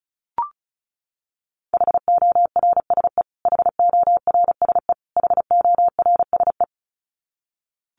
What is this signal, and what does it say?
Morse code, heard as a keyed tone, about 34 wpm, 700 Hz. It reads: HORSE HORSE HORSE
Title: Top 500 Words Repeated 35wpm